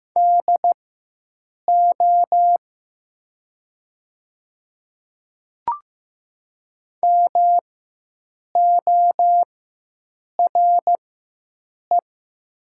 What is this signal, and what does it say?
Morse code, heard as a keyed tone, about 15 wpm, spaced as for 7 wpm, 700 Hz. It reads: DO MORE
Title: 4x Character Spacing - Top 100 Words - 15wpm